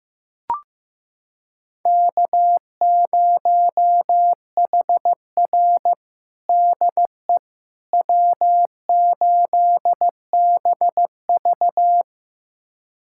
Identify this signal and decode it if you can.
Morse code, heard as a keyed tone, about 15 wpm, 700 Hz. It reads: K0HR DE W8BV
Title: USA Callsign DE Callsign - 4 Characters - 15wpm